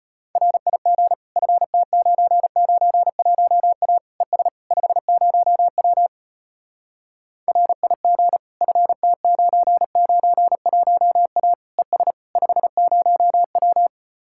Morse code, heard as a keyed tone, about 33 wpm, 700 Hz. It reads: RIG FT991A ES 50W RIG FT991A ES 50W